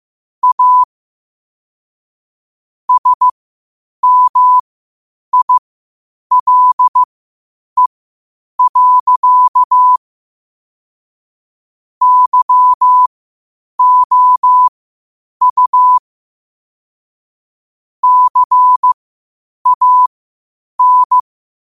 Morse code, heard as a keyed tone, about 15 wpm, spaced as for 8 wpm, 1000 Hz. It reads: A SMILE. YOU CAN
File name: qotdfiles / 8wpm